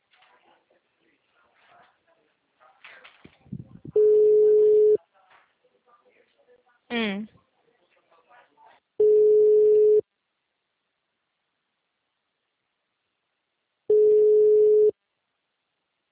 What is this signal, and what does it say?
Ringback tone (Europe, 425 Hz), with speech.
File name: out- -2012- - - .54805